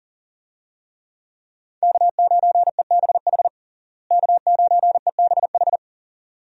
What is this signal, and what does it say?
Morse code, heard as a keyed tone, about 40 wpm, 700 Hz. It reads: K9EBH K9EBH
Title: USA Call Signs - 5 Characters - Repeated 2x - 40wpm